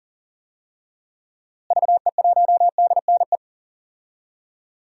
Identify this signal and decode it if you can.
Morse code, heard as a keyed tone, about 40 wpm, 700 Hz. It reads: VE1DNE
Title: DXCC-100 Call Signs - 6 Characters - Repeated 2x - 40wpm